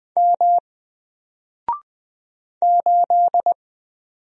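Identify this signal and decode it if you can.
Morse code, heard as a keyed tone, about 20 wpm, 700 Hz. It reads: M 8
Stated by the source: Single letter and number Morse drill at 20 wpm, from N4LCD.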